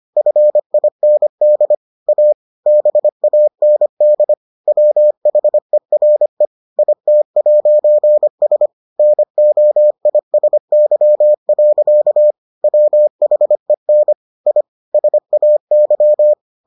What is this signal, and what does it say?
Morse code, heard as a keyed tone, about 25 wpm, 600 Hz. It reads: FIND A BAND WHERE IT'S NOISY. WHEN I SAY